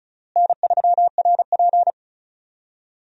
Morse code, heard as a keyed tone, about 36 wpm, 700 Hz. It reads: N3RP
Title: USA Call Signs - 4 Characters - Repeated 2x - 35wpm